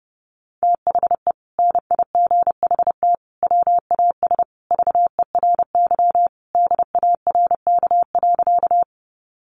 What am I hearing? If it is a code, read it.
Morse code, heard as a keyed tone, about 30 wpm, 700 Hz. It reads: THE NIGHT WAS VERY DARK.
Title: Sentences from Top 500 Words 30wpm